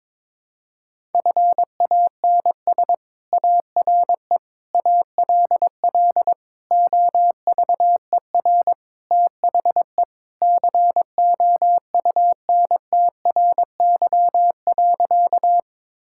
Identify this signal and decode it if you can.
Morse code, heard as a keyed tone, about 22 wpm, 700 Hz. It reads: FANS ARE ALL OVER THE COUNTRY.
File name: practice / 22wpm / Sentences with 4-Word Phrases - 22wpm